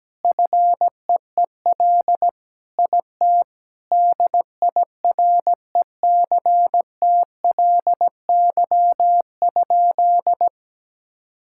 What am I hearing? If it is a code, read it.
Morse code, heard as a keyed tone, about 17 wpm, 700 Hz. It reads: FEEL IT DIRECTLY?
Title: Sentences from Top 1000 Words - Review Part 5 - 17wpm